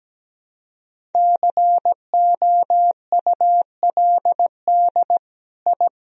Morse code, heard as a keyed tone, about 17 wpm, 700 Hz. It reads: COULD I
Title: Sentences from Top 1000 Words - Review Part 1 - 17wpm